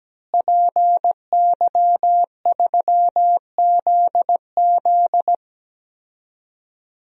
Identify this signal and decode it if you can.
Morse code, heard as a keyed tone, about 17 wpm, 700 Hz. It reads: PY3ZZ
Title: DXCC-100 Call Signs - 5 Characters - Repeated 2x - 17wpm